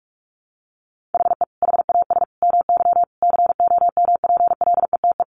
Morse code, heard as a keyed tone, about 44 wpm, 700 Hz. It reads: HE HAS MY COMPLETE
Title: Sentences from Top 1500 Words 45wpm